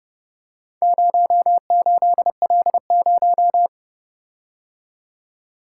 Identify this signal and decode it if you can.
Morse code, heard as a keyed tone, about 29 wpm, 700 Hz. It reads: 08L0